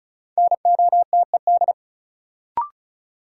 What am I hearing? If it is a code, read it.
Morse code, heard as a keyed tone, about 36 wpm, 700 Hz. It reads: NOTED
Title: Sets of 5 Words - New Word Drill - 35wpm